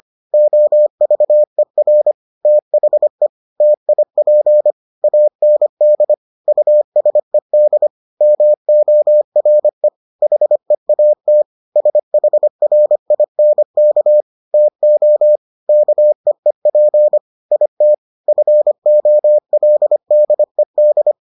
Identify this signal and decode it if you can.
Morse code, heard as a keyed tone, about 25 wpm, 600 Hz. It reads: OVER THE TIP AND USED MORE HEAT SHRINK TO KEEP IT FOLDED